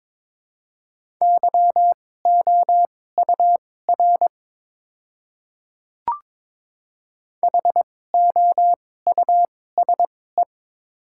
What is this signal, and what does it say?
Morse code, heard as a keyed tone, about 21 wpm, spaced as for 16 wpm, 700 Hz. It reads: YOUR HOUSE
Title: 2x Character Spacing - Top 200 Words - 22wpm